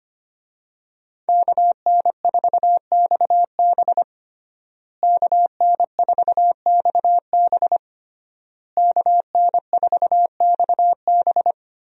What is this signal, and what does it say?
Morse code, heard as a keyed tone, about 25 wpm, 700 Hz. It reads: KN4XB KN4XB KN4XB
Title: USA Call Signs - 5 Characters - Repeated 3x - 25wpm